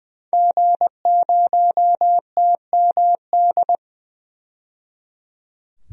Morse code, heard as a keyed tone, about 20 wpm, 700 Hz. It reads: G0TMD
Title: DXCC-100 Call Signs - 5 Characters - 20wpm